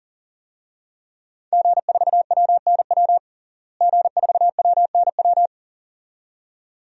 Morse code, heard as a keyed tone, about 40 wpm, 700 Hz. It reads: G4WNW G4WNW
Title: DXCC-100 Call Signs - 5 Characters - Repeated 2x - 40wpm